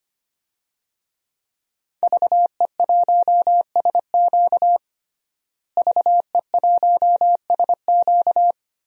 Morse code, heard as a keyed tone, about 25 wpm, 700 Hz. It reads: VE1SQ VE1SQ
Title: DXCC-100 Call Signs - 5 Characters - Repeated 2x - 25wpm